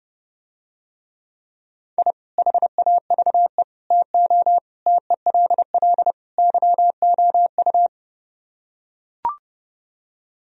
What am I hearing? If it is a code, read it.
Morse code, heard as a keyed tone, about 30 wpm, 700 Hz. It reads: I HAVE TO TELL YOU